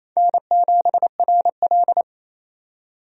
Morse code, heard as a keyed tone, about 28 wpm, 700 Hz. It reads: N7RL
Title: USA Call Signs - 4 Characters - 28wpm